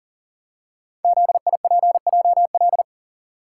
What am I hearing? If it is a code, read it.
Morse code, heard as a keyed tone, about 40 wpm, 700 Hz. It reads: ZIPJL
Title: Code Groups - 5 Letter:Numbers - 40wpm